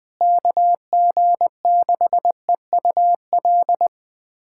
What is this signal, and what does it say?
Morse code, heard as a keyed tone, about 20 wpm, 700 Hz. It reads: KG6EUL
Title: USA Callsign DE Callsign - 6 Characters - 20wpm